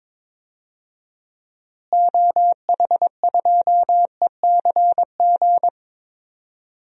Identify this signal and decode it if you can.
Morse code, heard as a keyed tone, about 22 wpm, 700 Hz. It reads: OH2ECG
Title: DXCC-100 Call Signs - 6 Characters - Repeated 3x - 22wpm